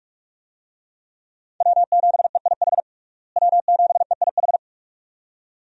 Morse code, heard as a keyed tone, about 48 wpm, 700 Hz. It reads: W7EIH W7EIH
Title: USA Call Signs - 5 Characters - Repeated 2x - 45wpm